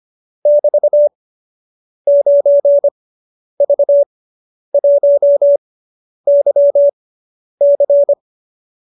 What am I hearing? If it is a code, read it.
Morse code, heard as a keyed tone, about 25 wpm, spaced as for 10 wpm, 590 Hz. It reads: <BT> 9V1YC